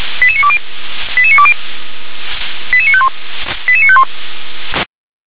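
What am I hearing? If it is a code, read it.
Five-tone selective call (ZVEI), heard as quick runs of five tones, tones of about 70 ms. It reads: 88920 88920 88951 88951
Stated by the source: ZVEI five-tone selective call received on an SDR.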